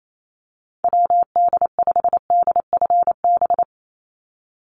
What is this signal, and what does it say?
Morse code, heard as a keyed tone, about 28 wpm, 700 Hz. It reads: WD5DFB
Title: USA Call Signs 28wpm